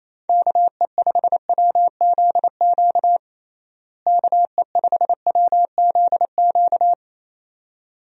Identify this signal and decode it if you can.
Morse code, heard as a keyed tone, about 28 wpm, 700 Hz. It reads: KE5WZQ KE5WZQ
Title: USA Call Signs - 6 Characters - Repeated 2x - 28wpm